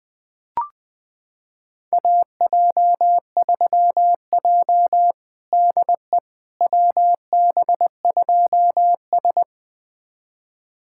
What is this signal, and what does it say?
Morse code, heard as a keyed tone, about 21 wpm, 700 Hz. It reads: AJ3J DE WB2S
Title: USA Callsign DE Callsign - 4 Characters - 20wpm